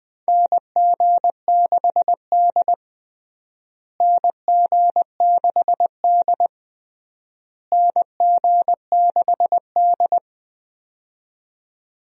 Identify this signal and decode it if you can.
Morse code, heard as a keyed tone, about 21 wpm, 700 Hz. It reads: NG6D NG6D NG6D
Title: USA Call Signs - 4 Characters - Repeated 3x - 20wpm